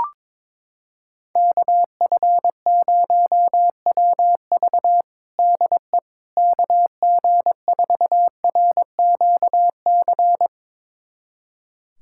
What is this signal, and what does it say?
Morse code, heard as a keyed tone, about 22 wpm, 700 Hz. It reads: KF0WV DE KG4RQC
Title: USA Callsign DE Callsign 22wpm